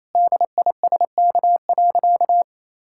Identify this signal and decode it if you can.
Morse code, heard as a keyed tone, about 28 wpm, 700 Hz. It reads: DISK.
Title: Sentences from Top 1000 Words - Review Part 6 - 28wpm